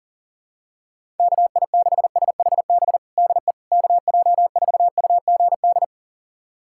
Morse code, heard as a keyed tone, about 37 wpm, 700 Hz. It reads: KI6SHB DE KJ4UGD
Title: USA Callsign DE Callsign 40wpm